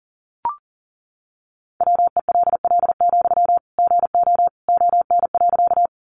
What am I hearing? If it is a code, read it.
Morse code, heard as a keyed tone, about 40 wpm, 700 Hz. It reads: WELL, GO ON.
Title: Sentences from Top 200 Words 40wpm